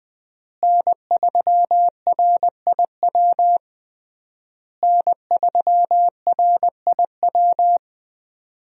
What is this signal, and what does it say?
Morse code, heard as a keyed tone, about 21 wpm, 700 Hz. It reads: N3RIW N3RIW
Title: USA Call Signs - 5 Characters - Repeated 3x - 20wpm